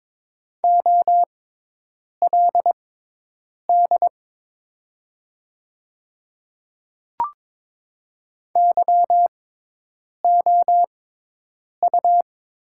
Morse code, heard as a keyed tone, about 22 wpm, spaced as for 8 wpm, 700 Hz. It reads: OLD YOU
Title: 6x Character Spacing - Top 200 Words - 22wpm